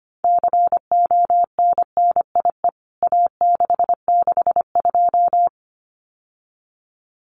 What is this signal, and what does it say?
Morse code, heard as a keyed tone, about 25 wpm, 700 Hz. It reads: CONNIE A662
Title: CWT Exchange 25wpm